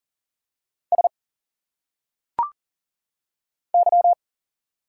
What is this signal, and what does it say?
Morse code, heard as a keyed tone, about 40 wpm, 700 Hz. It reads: S Y
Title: Single Letters - Phonetic Pronunciation 40wpm